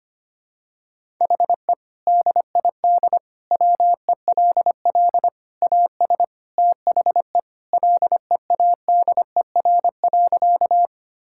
Morse code, heard as a keyed tone, about 25 wpm, 700 Hz. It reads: HE DID WELL AS THE LEADER.
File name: practice / 25wpm / Sentences with 3-Word Phrases - 25wpm